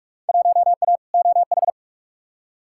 Morse code, heard as a keyed tone, about 43 wpm, 700 Hz. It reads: 1A OH
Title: Field Day Exchanges - 45wpm